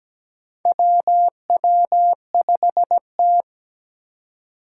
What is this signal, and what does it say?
Morse code, heard as a keyed tone, about 17 wpm, 700 Hz. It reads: WW5T
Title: USA Call Signs - 4 Characters - 2x Character Spacing - 17wpm